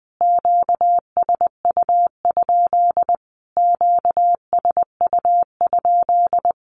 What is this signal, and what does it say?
Morse code, heard as a keyed tone, about 20 wpm, 700 Hz. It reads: QSU? QSU?
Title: Q-Codes Repeated 3x 20wpm